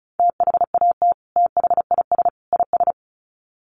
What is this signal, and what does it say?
Morse code, heard as a keyed tone, about 34 wpm, 700 Hz. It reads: THAT THIS IS
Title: Sets of 3 Words 35wpm